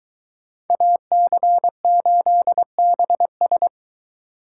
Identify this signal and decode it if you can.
Morse code, heard as a keyed tone, about 22 wpm, 700 Hz. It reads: AC8BS